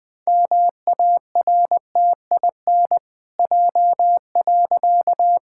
Morse code, heard as a keyed tone, about 20 wpm, 700 Hz. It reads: MARTIN J.
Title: Pangram Challenge v2 - 20wpm